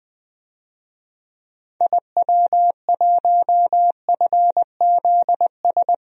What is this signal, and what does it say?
Morse code, heard as a keyed tone, about 20 wpm, 700 Hz. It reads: IW1FZS
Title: DXCC-100 Call Signs - 6 Characters - Repeated 2x - 20wpm